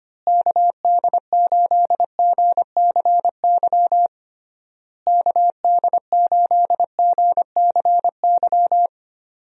Morse code, heard as a keyed tone, about 25 wpm, 700 Hz. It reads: KD8GCY KD8GCY
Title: USA Call Signs - 6 Characters - Repeated 3x - 25wpm